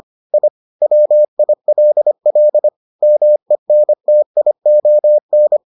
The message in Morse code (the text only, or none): I WILL MENTION